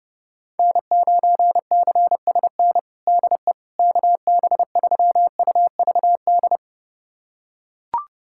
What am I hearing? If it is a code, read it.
Morse code, heard as a keyed tone, about 32 wpm, 700 Hz. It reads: N9CSN DE KB3UVD
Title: USA Callsign DE Callsign 30wpm